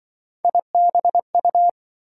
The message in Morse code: IBU

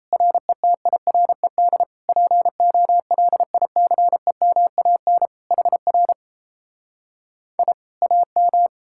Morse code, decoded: RETIRED POLICEMAN HR I AM

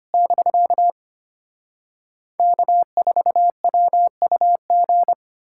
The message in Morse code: <BK> K4WUG

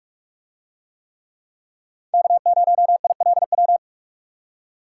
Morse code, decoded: K0ILW

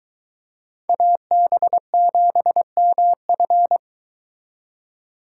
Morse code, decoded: AB7MF

